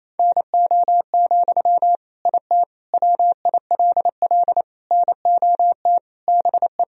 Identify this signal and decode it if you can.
Morse code, heard as a keyed tone, about 28 wpm, 700 Hz. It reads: NO, IT WILL NOT BE
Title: Sentences from Top 1000 Words - Review Part 4 - 28wpm